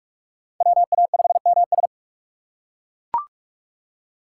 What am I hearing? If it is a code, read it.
Morse code, heard as a keyed tone, about 41 wpm, 700 Hz. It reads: WA5MS